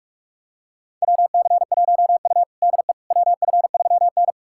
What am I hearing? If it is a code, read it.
Morse code, heard as a keyed tone, about 44 wpm, 690 Hz. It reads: WC1U DE WF3N
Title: USA Callsign DE Callsign - 4 Characters - 45wpm